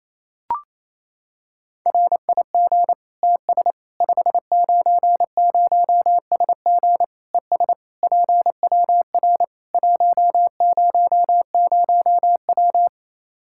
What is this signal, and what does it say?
Morse code, heard as a keyed tone, about 28 wpm, 700 Hz. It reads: RIG TS 590SG ES PWR 100W
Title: K1USN SST QSO Practice 28wpm